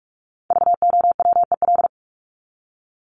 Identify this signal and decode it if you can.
Morse code, heard as a keyed tone, about 44 wpm, 700 Hz. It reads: VOWEL
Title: Top 500 Words 45wpm